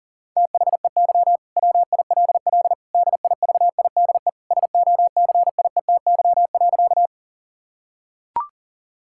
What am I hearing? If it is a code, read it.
Morse code, heard as a keyed tone, about 37 wpm, 700 Hz. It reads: THEY WILL DIVIDE SOCIETY.
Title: Sentences from Top 1600 Words 40wpm